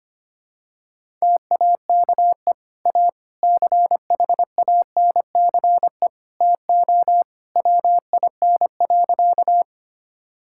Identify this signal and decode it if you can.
Morse code, heard as a keyed tone, about 25 wpm, 700 Hz. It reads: TAKE A CHANCE TO WIN.
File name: practice / 25wpm / Sentences with 3-Word Phrases - 25wpm